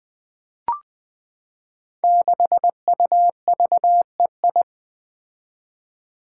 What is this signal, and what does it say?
Morse code, heard as a keyed tone, about 20 wpm, 700 Hz. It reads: 6UVEI
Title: Five Challenging Letter-Number Combinations 20wpm